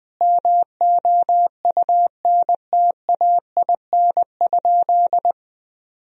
Morse code, heard as a keyed tone, about 21 wpm, 700 Hz. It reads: MOUNTAIN?
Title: Sentences from Top 1000 Words - Review Part 1 - 20wpm